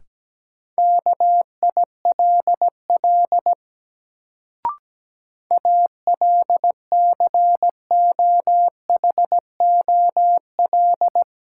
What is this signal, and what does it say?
Morse code, heard as a keyed tone, about 17 wpm, 700 Hz. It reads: KILL ALCOHOL
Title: Sentences from Top 1400 Words - New Word Drill Alt - 17wpm